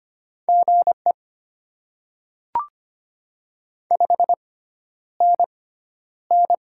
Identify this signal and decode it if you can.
Morse code, heard as a keyed tone, about 25 wpm, spaced as for 11 wpm, 700 Hz. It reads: GE 5NN
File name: Top 20 QSO Elements - 6x Character Spacing - 25wpm